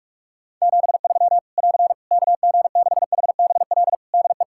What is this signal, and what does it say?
Morse code, heard as a keyed tone, about 46 wpm, 700 Hz. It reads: 73 <AR> KG6HBL DE
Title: Practice QSO - Exchange 4 - Ending - 45wpm